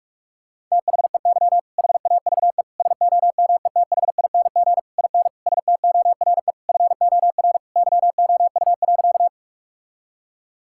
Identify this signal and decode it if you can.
Morse code, heard as a keyed tone, about 41 wpm, 700 Hz. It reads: THEY HAVE SOMETHING IN STORE FOR YOU.
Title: Sentences from Top 1000 Words - Review Part 5 - 45wpm